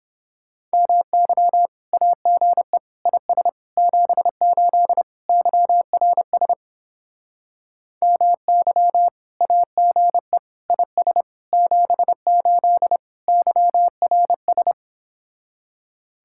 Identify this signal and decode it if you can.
Morse code, heard as a keyed tone, about 31 wpm, 700 Hz. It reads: MY AGE IS 78 YRS MY AGE IS 78 YRS